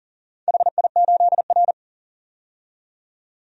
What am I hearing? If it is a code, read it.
Morse code, heard as a keyed tone, about 38 wpm, 700 Hz. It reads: HI8R
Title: DXCC-100 Call Signs - 4 Characters - 40wpm